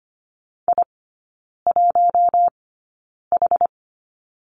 Morse code, heard as a keyed tone, about 25 wpm, spaced as for 9 wpm, 700 Hz. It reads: I1H
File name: course / 25:10wpm / 093 Lesson 16e - Letter c - Three Characters 25wpm